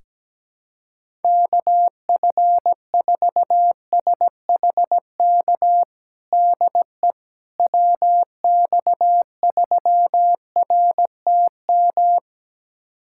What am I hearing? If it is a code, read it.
Morse code, heard as a keyed tone, about 17 wpm, 700 Hz. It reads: KF4SHK DE WX3RTM
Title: USA Callsign DE Callsign - 6 Characters - 17wpm